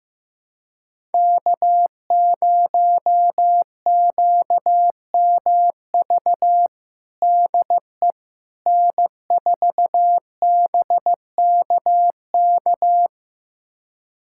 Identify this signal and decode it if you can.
Morse code, heard as a keyed tone, about 15 wpm, 700 Hz. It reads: K0QMV DE N4BKK
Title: USA Callsign DE Callsign - 5 Characters - 15wpm